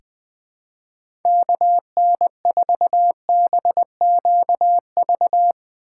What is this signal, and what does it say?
Morse code, heard as a keyed tone, about 21 wpm, 700 Hz. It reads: KN4BQV
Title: USA Call Signs - 6 Characters - 4x Character Spacing - 20wpm